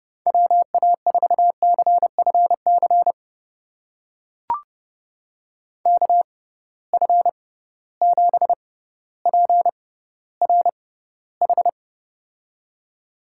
Morse code, heard as a keyed tone, about 29 wpm, spaced as for 15 wpm, 700 Hz. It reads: WA4CFC KF7PRH
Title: USA Call Signs - 6 Characters - 6x Character Spacing - 30wpm